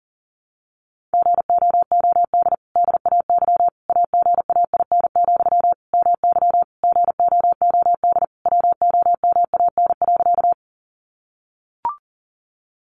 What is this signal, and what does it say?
Morse code, heard as a keyed tone, about 40 wpm, 700 Hz. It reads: GOOD DAY AGAIN, MY GOOD WOMAN.
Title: Sentences from Top 1100 Words 40wpm